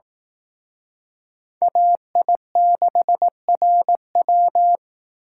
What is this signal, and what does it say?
Morse code, heard as a keyed tone, about 18 wpm, 700 Hz. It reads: AI6RW
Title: Speed Racing - USA Call Signs 15wpm